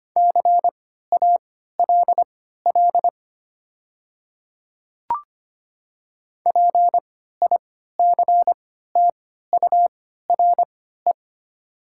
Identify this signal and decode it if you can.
Morse code, heard as a keyed tone, about 25 wpm, spaced as for 14 wpm, 700 Hz. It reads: CALL PICTURE
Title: 3x Character Spacing - Top 500 Words - 25wpm